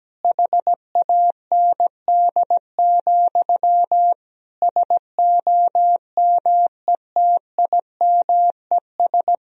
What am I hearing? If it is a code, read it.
Morse code, heard as a keyed tone, about 17 wpm, 700 Hz. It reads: HAND, SOMETIMES
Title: Sentences from Top 1000 Words - Review Part 3 - 17wpm